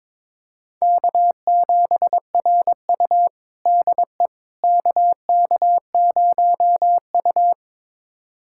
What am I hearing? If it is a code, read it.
Morse code, heard as a keyed tone, about 22 wpm, 700 Hz. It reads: K7RU DE KK0U
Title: USA Callsign DE Callsign - 4 Characters - 22wpm